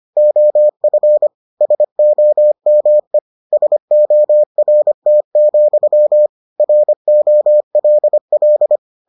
Morse code, heard as a keyed tone, about 25 wpm, 600 Hz. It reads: OF SOME SORT, ROLL